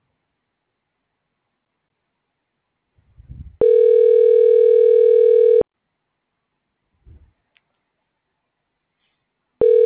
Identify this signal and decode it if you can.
Ringback tone (North America), 440 + 480 Hz.
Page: Call disconnected before getting connected